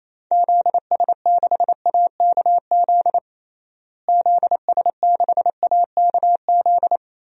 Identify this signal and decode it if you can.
Morse code, heard as a keyed tone, about 28 wpm, 700 Hz. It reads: ZS6AKZ ZS6AKZ
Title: DXCC-100 Call Signs - 6 Characters - Repeated 3x - 28wpm